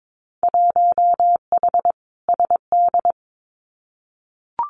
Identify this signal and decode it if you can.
Morse code, heard as a keyed tone, about 22 wpm, 700 Hz. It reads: 1H SD
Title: Winter Field Day Exchanges 22wpm